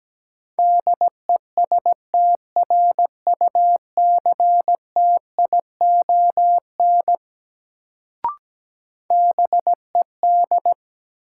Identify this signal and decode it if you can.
Morse code, heard as a keyed tone, about 17 wpm, 700 Hz. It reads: DESTRUCTION BED
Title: Sets of 5 Words Encore - New Word Drill - 17wpm